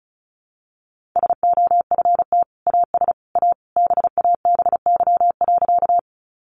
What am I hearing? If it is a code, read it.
Morse code, heard as a keyed tone, about 34 wpm, 700 Hz. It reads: SOFT AS A BABY.
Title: Sentences from Top 1500 Words 35wpm